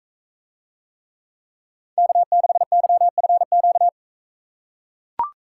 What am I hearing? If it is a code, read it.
Morse code, heard as a keyed tone, about 39 wpm, 700 Hz. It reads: K6YFQ